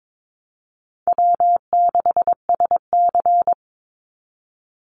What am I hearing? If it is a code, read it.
Morse code, heard as a keyed tone, about 22 wpm, 700 Hz. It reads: W6SC
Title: Easy Call Signs - 22wpm